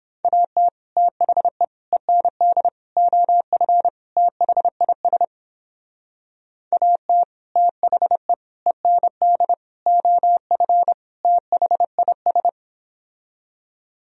Morse code, heard as a keyed tone, about 29 wpm, 700 Hz. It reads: AT THE END OF THIS AT THE END OF THIS